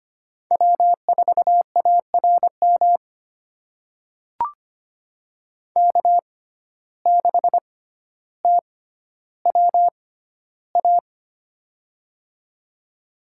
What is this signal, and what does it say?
Morse code, heard as a keyed tone, about 25 wpm, spaced as for 13 wpm, 700 Hz. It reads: W4ARM K6TWA
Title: USA Call Signs - 5 Characters - 6x Character Spacing - 25wpm